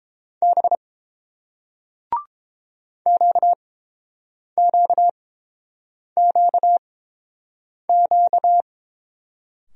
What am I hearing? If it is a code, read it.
Morse code, heard as a keyed tone, about 30 wpm, spaced as for 7 wpm, 700 Hz. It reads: B QQQQ